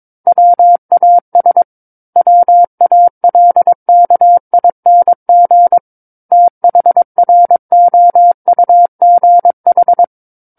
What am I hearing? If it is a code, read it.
Morse code, heard as a keyed tone, about 22 wpm, 700 Hz. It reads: WAS WALKING THROUGH
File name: Book One / 13-Chapter 13 - 22 WPM